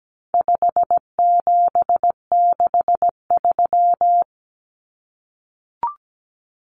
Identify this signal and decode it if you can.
Morse code, heard as a keyed tone, about 17 wpm, 700 Hz. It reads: 5763
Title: Numbers - 4-Digit 17wpm